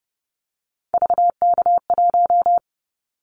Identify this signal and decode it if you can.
Morse code, heard as a keyed tone, about 30 wpm, 700 Hz. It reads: VK1
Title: Callsign Prefixes for Top 50 DXCC - 30wpm